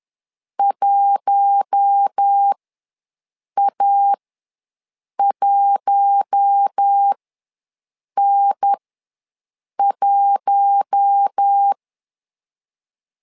Morse code, heard as a keyed tone, about 10 wpm, spaced as for 6 wpm, 800 Hz. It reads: 1A1N1
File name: LEZIONI / 4 Q1 ws 10wpm 2538